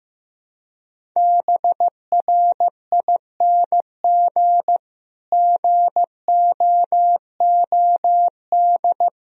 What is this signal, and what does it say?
Morse code, heard as a keyed tone, about 15 wpm, 700 Hz. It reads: BRING GOOD
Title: Sentences from Top 1000 Words - Review Part 4 - 15wpm